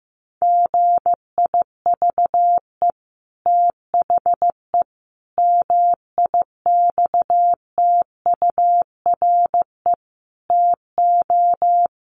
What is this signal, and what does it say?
Morse code, heard as a keyed tone, about 14 wpm, 700 Hz. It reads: GIVE THE MIXTURE TO
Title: Sentences from Top 1500 Words 15wpm